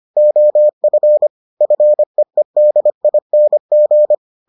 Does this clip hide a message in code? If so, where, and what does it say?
Yes: Morse code from 0.2 to 4.1 s, heard as a keyed tone, about 25 wpm, 600 Hz. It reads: OF FEEDING